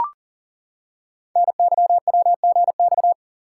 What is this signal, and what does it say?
Morse code, heard as a keyed tone, about 40 wpm, 690 Hz. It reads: NYWGX
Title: Code Groups - 5 Letter:Numbers - 40wpm